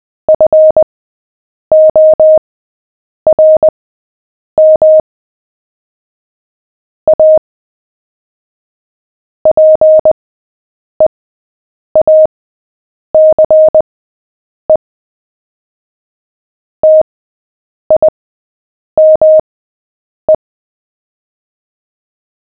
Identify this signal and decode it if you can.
Morse code, heard as a keyed tone, about 20 wpm, spaced as for 8 wpm, 620 Hz. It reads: FORM A PEACE TIME
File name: cw / fists / Short-Story-7-8wpm